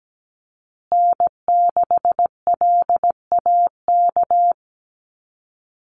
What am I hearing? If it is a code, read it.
Morse code, heard as a keyed tone, about 17 wpm, 700 Hz. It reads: N6LAK
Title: USA Call Signs Repeated 2x 17wpm